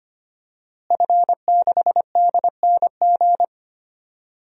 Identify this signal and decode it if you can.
Morse code, heard as a keyed tone, about 25 wpm, 700 Hz. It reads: F6DNG
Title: DXCC-100 Call Signs - 5 Characters - Repeated 3x - 25wpm